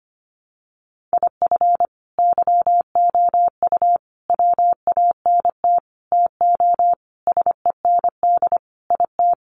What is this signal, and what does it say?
Morse code, heard as a keyed tone, about 25 wpm, 700 Hz. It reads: IF YOU WANT TO SEND IT
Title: Sentences from Top 1500 Words 25wpm